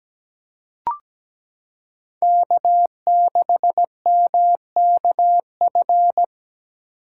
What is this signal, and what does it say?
Morse code, heard as a keyed tone, about 17 wpm, 700 Hz. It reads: K6MKF
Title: K1USN SST QSO Practice - 17wpm